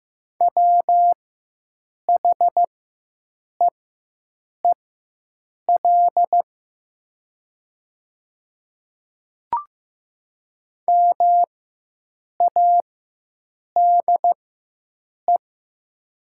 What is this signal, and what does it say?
Morse code, heard as a keyed tone, about 15 wpm, spaced as for 7 wpm, 700 Hz. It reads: WHEEL MADE
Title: 4x Character Spacing - Top 500 Words - 15wpm